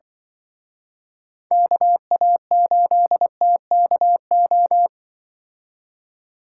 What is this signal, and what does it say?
Morse code, heard as a keyed tone, about 24 wpm, 700 Hz. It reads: KA8TKO
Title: Speed Racing - USA 2x3 Call Signs 20wpm